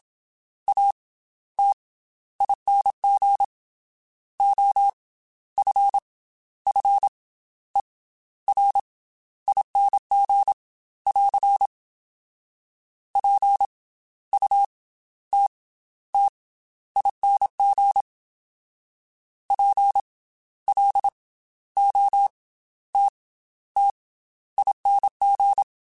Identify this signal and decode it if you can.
Morse code, heard as a keyed tone, about 27 wpm, spaced as for 13 wpm, 800 Hz. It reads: ATING OFFERING <AR> PUTTING PLOTTING
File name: Special INGs / Special INGs-mp3s / Slow ing @25